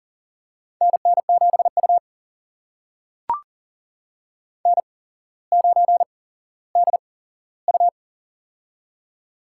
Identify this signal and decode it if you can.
Morse code, heard as a keyed tone, about 40 wpm, spaced as for 17 wpm, 700 Hz. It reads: NN7U N9DU
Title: USA Call Signs - 4 Characters - 8x Character Spacing - 40wpm